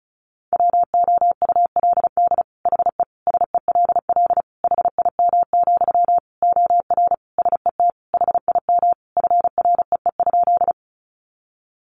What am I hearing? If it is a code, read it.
Morse code, heard as a keyed tone, about 34 wpm, 700 Hz. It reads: WOULD HE SELL HIM, OR SET HIM FREE?
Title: Sentences from Top 1100 Words 35wpm